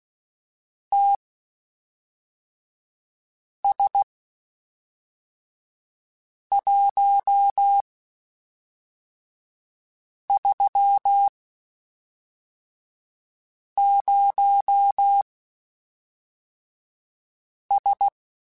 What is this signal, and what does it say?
Morse code, heard as a keyed tone, about 16 wpm, spaced as for 3 wpm, 800 Hz. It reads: TS130S
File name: QSOs / QSOs-mp3s / WD0BGP 1@5